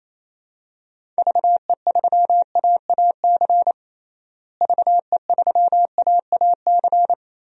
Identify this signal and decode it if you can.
Morse code, heard as a keyed tone, about 28 wpm, 700 Hz. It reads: VE3AAC VE3AAC
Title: DXCC-100 Call Signs - 6 Characters - Repeated 3x - 28wpm